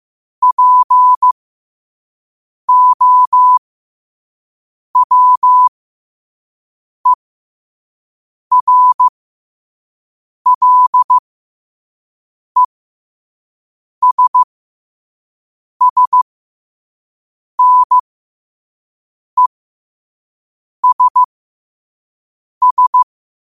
Morse code, heard as a keyed tone, about 15 wpm, spaced as for 5 wpm, 1000 Hz. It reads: POWERLESSNESS
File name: qotdfiles / 5wpm